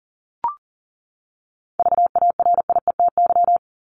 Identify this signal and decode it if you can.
Morse code, heard as a keyed tone, about 40 wpm, 700 Hz. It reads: VARIETY
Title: Top 501-750 Words 40wpm